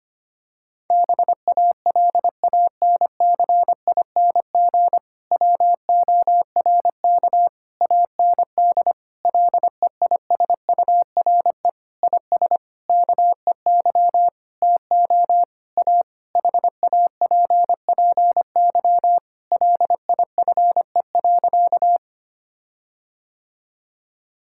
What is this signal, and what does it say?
Morse code, heard as a keyed tone, about 25 wpm, 700 Hz. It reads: BALANCING WORK AND LEISURE IS KEY TO A HAPPY LIFE.
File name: practice / 25wpm / Difficult Sentences with Hints - 25wpm